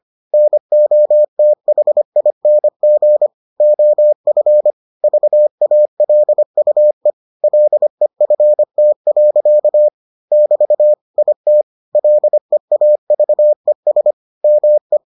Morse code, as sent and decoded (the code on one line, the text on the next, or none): -. --- - .... .. -. --. / --- ..-. / ...- .- .-.. ..- . / .-.. ..-. - .-.-.- / -...- / .. - / .-.. .- ...- . ... / -- .
NOTHING OF VALUE LEFT. <BT> IT LEAVES ME